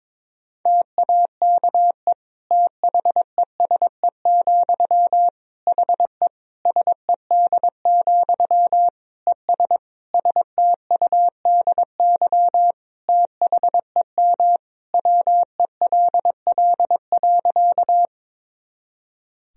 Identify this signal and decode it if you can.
Morse code, heard as a keyed tone, about 22 wpm, 700 Hz. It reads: TAKE THESE, HE SED, ES STUDY THEM WELL.
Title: Sentences from Top 300 Words and Abbreviations 22wpm